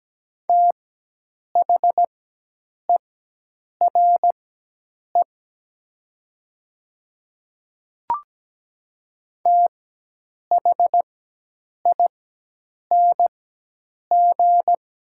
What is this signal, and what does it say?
Morse code, heard as a keyed tone, about 17 wpm, spaced as for 8 wpm, 700 Hz. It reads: THERE THING
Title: 4x Character Spacing - Top 100 Words - 17wpm